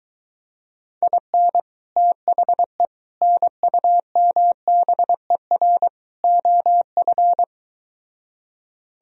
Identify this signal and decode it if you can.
Morse code, heard as a keyed tone, about 23 wpm, 700 Hz. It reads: IN THE NUMBER OF